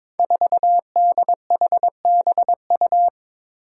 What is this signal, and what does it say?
Morse code, heard as a keyed tone, about 22 wpm, 700 Hz. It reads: 4DHBU
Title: Five Challenging Letter-Number Combinations 22wpm